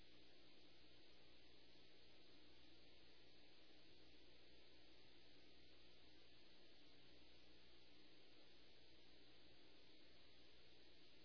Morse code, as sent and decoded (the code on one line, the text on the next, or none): none